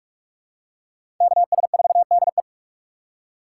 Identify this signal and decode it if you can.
Morse code, heard as a keyed tone, about 45 wpm, 700 Hz. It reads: KS4DE
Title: USA Call Signs - 4x Character Spacing - 45wpm